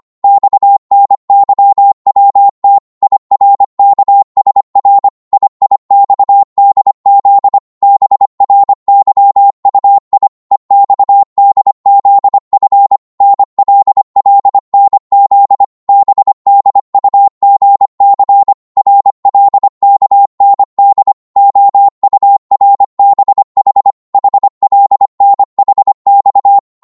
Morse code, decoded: XNYWT IRKSR IIXDZ BRYUI EXDZF NLLNZ BDUGC RLKND OURBH HLNHX